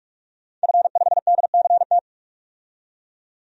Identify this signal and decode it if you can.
Morse code, heard as a keyed tone, about 46 wpm, 700 Hz. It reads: F5DCT
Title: DXCC-100 Call Signs - 5 Characters - 45wpm